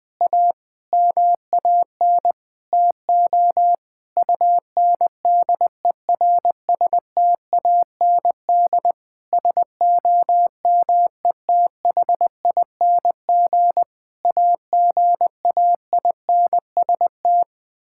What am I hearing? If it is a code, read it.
Morse code, heard as a keyed tone, about 20 wpm, 700 Hz. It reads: A MAN TO UNDERSTAND SOMETHING AGAINST
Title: Sentences with 9-Word Phrases - Spoken Beginning - 20wpm